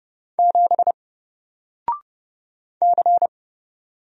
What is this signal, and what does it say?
Morse code, heard as a keyed tone, about 30 wpm, 700 Hz. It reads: 7 C
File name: Single-Letter-Number-30wpm